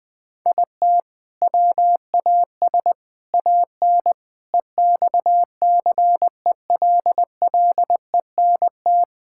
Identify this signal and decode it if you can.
Morse code, heard as a keyed tone, about 21 wpm, 700 Hz. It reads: IT WAS AN EXCELLENT
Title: Sentences from Top 1000 Words - Review Part 5 - 20wpm